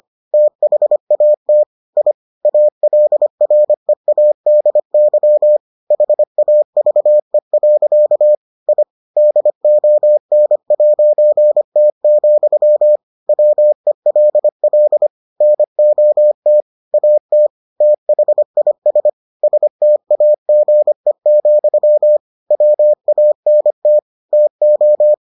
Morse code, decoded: THAT I ALREADY HAVE. I DON'T, WELL NOT AT THIS STAGE, WANT TO

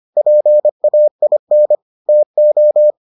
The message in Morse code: PAIN TO

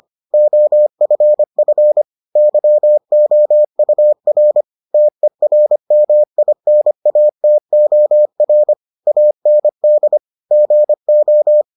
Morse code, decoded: OFF YOUR TERMINATOR AND GO

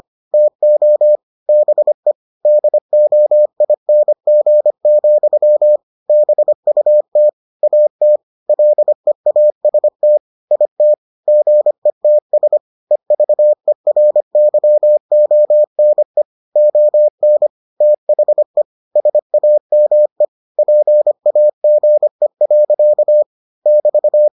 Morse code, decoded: TO BE DOING, BUT AT LEAST IT GETS EVERYONE ON THE SAME PAGE. <BT>

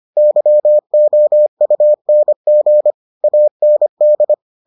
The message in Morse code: YOUNG AND

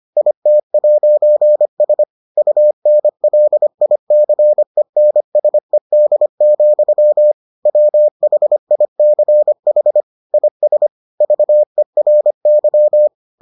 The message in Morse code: IT'S UNLICENSED, WHICH IS VERY

